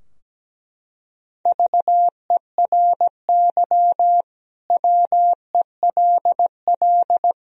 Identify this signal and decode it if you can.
Morse code, heard as a keyed tone, about 17 wpm, 700 Hz. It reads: VERY WELL
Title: Sentences from Top 1000 Words - Review Part 3 - 17wpm